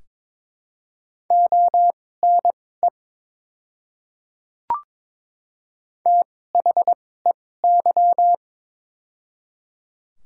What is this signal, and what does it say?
Morse code, heard as a keyed tone, about 22 wpm, spaced as for 16 wpm, 700 Hz. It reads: ONE THEY
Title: 2x Character Spacing - Top 100 Words - 22wpm